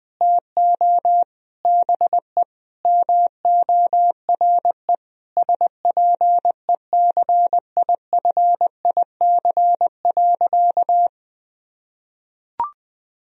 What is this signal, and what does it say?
Morse code, heard as a keyed tone, about 20 wpm, 700 Hz. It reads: TO BE MORE SPECIFIC.
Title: Sentences from Top 1000 Words - Review Part 3 - 20wpm